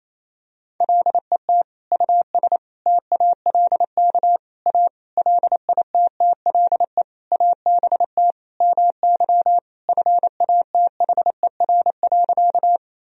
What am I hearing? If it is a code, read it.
Morse code, heard as a keyed tone, about 27 wpm, 700 Hz. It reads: LET US TALK A LITTLE ABT MY FATHER.
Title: Sentences from Top 400 Words and Abbreviations 28wpm